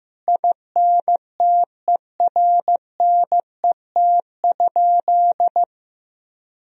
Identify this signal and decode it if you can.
Morse code, heard as a keyed tone, about 16 wpm, 700 Hz. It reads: INTERNET?
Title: Sentences from Top 1000 Words - Review Part 4 - 15wpm